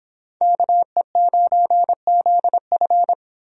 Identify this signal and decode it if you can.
Morse code, heard as a keyed tone, about 30 wpm, 700 Hz. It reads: KE9ZF